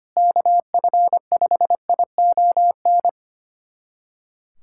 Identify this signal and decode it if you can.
Morse code, heard as a keyed tone, about 25 wpm, 700 Hz. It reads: KF5ION
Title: USA Callsign DE Callsign - 6 Characters - 25wpm